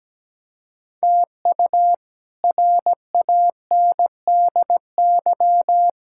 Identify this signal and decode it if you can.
Morse code, heard as a keyed tone, about 17 wpm, 700 Hz. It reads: TU RANDY
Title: K1USN SST QSO Practice - 17wpm